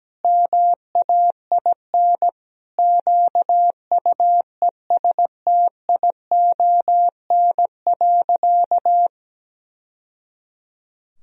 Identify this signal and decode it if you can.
Morse code, heard as a keyed tone, about 17 wpm, 700 Hz. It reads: MAIN QUESTION.
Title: Sentences from Top 1000 Words - Review Part 2 - 17wpm